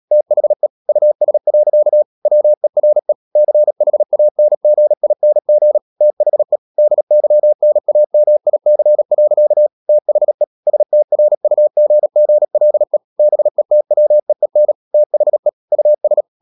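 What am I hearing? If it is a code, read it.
Morse code, heard as a keyed tone, about 37 wpm, 600 Hz. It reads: THE US. WERE CHANGING THE DYNAMIC. THE STRUGGLE BETWEEN THE US
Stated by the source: Morse code practice at 35–39 wpm sending English text of Ken Follett's novel Never, from CW med Gnister.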